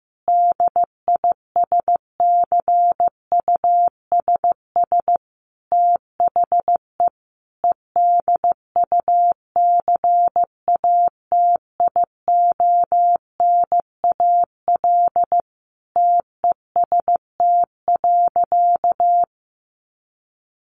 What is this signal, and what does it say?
Morse code, heard as a keyed tone, about 15 wpm, 700 Hz. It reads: DISCUSS THE EDUCATIONAL TEST.
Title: Sentences from Top 1200 Words 15wpm